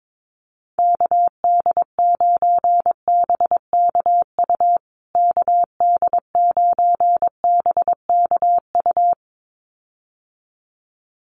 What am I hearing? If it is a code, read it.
Morse code, heard as a keyed tone, about 22 wpm, 700 Hz. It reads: KD9BKU KD9BKU
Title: USA Call Signs Repeated 2x 22wpm